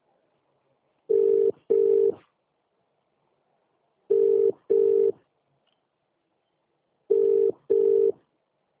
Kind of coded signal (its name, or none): telephone call progress tone